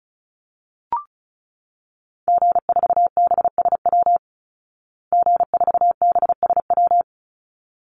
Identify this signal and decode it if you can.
Morse code, heard as a keyed tone, about 34 wpm, 700 Hz. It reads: G4BSW G4BSW
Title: Call Signs Repeated 3x - 35wpm